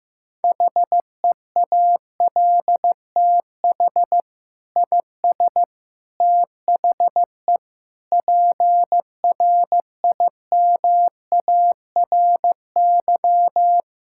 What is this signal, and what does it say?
Morse code, heard as a keyed tone, about 15 wpm, 700 Hz. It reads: HEALTH IS THE PRIMARY
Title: Sentences from Top 1000 Words - Review Part 5 - 15wpm